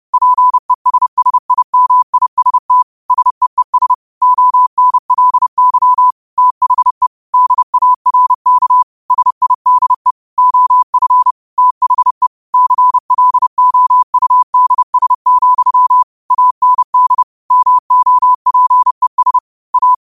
Morse code, heard as a keyed tone, about 30 wpm, 1000 Hz. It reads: PESSIMIST SEES ONLY THE DARK SIDE OF THE CLOUDS, AND MOPES A
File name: qotdfiles / 30wpm